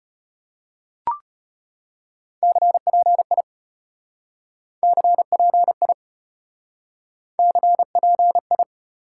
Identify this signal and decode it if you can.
Morse code, heard as a keyed tone, about 34 wpm, 700 Hz. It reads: CPI CPI CPI